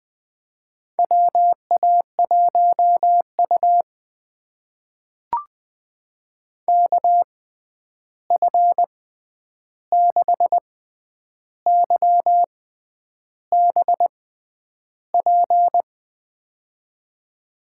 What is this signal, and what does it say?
Morse code, heard as a keyed tone, about 20 wpm, spaced as for 9 wpm, 700 Hz. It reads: WA1U KF6YBP
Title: USA Call Signs - 6x Character Spacing - 20wpm